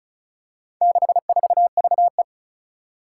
Morse code, heard as a keyed tone, about 36 wpm, 700 Hz. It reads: 64VE
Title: Four Challenging Letter-Number Combinations 35wpm